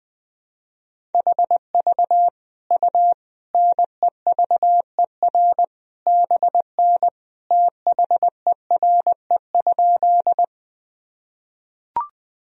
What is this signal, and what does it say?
Morse code, heard as a keyed tone, about 20 wpm, 700 Hz. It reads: HV U NEVER BN THERE?
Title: Sentences from Top 300 Words and Abbreviations 20wpm